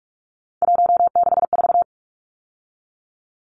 Morse code, heard as a keyed tone, about 44 wpm, 700 Hz. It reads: 164
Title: Numbers - 3-Digit 45wpm